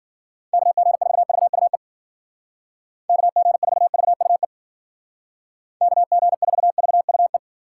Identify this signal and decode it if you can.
Morse code, heard as a keyed tone, about 52 wpm, 700 Hz. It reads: KG4VUE KG4VUE KG4VUE